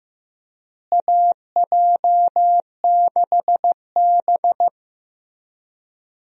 Morse code, heard as a keyed tone, about 15 wpm, 700 Hz. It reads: AJ6B
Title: USA Call Signs - 4 Characters - Repeated 3x - 15wpm